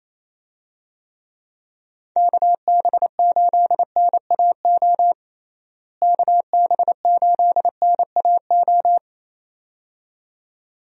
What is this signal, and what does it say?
Morse code, heard as a keyed tone, about 28 wpm, 700 Hz. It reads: KB8NAO KB8NAO
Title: USA Call Signs - 6 Characters - Repeated 2x - 28wpm